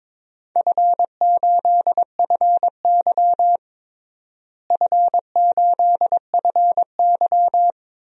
Morse code, heard as a keyed tone, about 22 wpm, 700 Hz. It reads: F8FY F8FY
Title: DXCC-100 Call Signs - 4 Characters - Repeated 3x - 22wpm